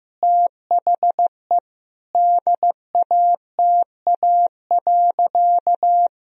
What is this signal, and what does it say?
Morse code, heard as a keyed tone, about 15 wpm, 700 Hz. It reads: THE DATA.
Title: Sentences from Top 1000 Words - Review Part 3 - 15wpm